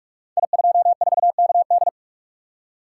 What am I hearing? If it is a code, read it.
Morse code, heard as a keyed tone, about 44 wpm, 700 Hz. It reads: I24KD
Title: Code Groups - 5 Letter:Numbers - 45wpm